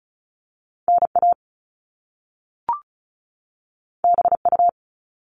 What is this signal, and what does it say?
Morse code, heard as a keyed tone, about 36 wpm, 700 Hz. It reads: NA BU
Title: Top Two Letter Combinations in Words 35wpm